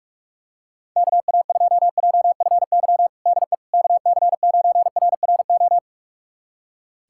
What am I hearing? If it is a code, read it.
Morse code, heard as a keyed tone, about 46 wpm, 700 Hz. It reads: KA2JFY DE KC9RRO
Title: USA Callsign DE Callsign - 6 Characters - 45wpm